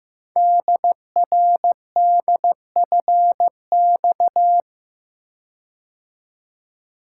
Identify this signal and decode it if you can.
Morse code, heard as a keyed tone, about 15 wpm, 700 Hz. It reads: DRDFX
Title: Code Groups - 5 Letters - 15wpm